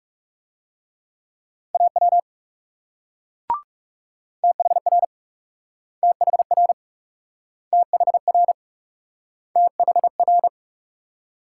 Morse code, heard as a keyed tone, about 41 wpm, 700 Hz. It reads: AW THR THR THR THR